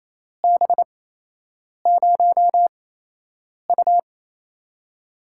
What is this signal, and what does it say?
Morse code, heard as a keyed tone, about 28 wpm, spaced as for 8 wpm, 700 Hz. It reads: B0U
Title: USA Call Signs - 8x Character Spacing - 28wpm